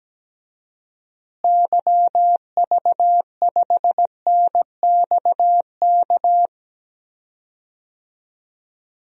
Morse code, heard as a keyed tone, about 17 wpm, 700 Hz. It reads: YV5NXK
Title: DXCC-100 Call Signs - 6 Characters - Repeated 2x - 17wpm